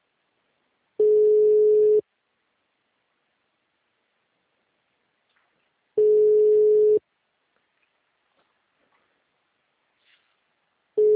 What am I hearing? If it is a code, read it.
Ringback tone (Europe, 425 Hz).
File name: out- -2005- - - .347